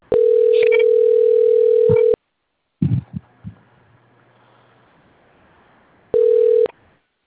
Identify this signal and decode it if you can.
Ringback tone (North America), 440 + 480 Hz.